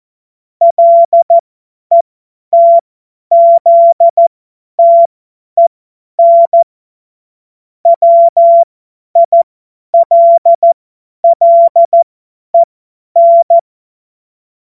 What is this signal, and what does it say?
Morse code, heard as a keyed tone, about 14 wpm, spaced as for 10 wpm, 680 Hz. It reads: LETZTEN WILLEN
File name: Morsekurs / Witze / Witz 17. Mai 2022 50 F70